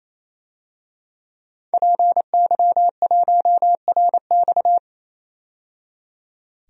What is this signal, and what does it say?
Morse code, heard as a keyed tone, about 27 wpm, 700 Hz. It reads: PY1RX
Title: DXCC-100 Call Signs - 5 Characters - 28wpm